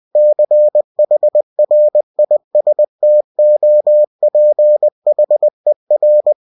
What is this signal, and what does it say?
Morse code, heard as a keyed tone, about 20 wpm, 600 Hz. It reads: CHRISTOPHER
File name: cw-books / winnie-the-pooh / winnie-20wpm-page-0052